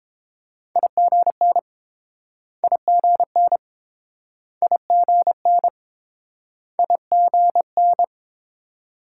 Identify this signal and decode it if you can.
Morse code, heard as a keyed tone, about 31 wpm, 700 Hz. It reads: IGN IGN IGN IGN